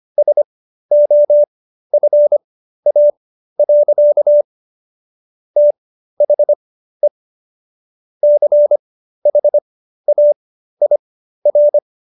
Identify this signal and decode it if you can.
Morse code, heard as a keyed tone, about 25 wpm, spaced as for 13 wpm, 590 Hz. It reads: SOFA. THE CHAIR